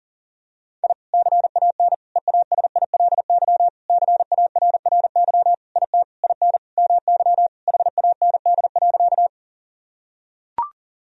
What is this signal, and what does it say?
Morse code, heard as a keyed tone, about 40 wpm, 700 Hz. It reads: I CAN EASILY CARRY IT IN MY HAND.
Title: Sentences from Top 1000 Words - Review Part 1 - 40wpm